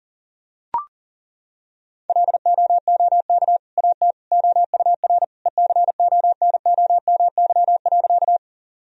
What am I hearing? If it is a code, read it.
Morse code, heard as a keyed tone, about 40 wpm, 700 Hz. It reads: LOOK AT OUR ECONOMY.
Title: Sentences from Top 1000 Words - Review Part 3 - 40wpm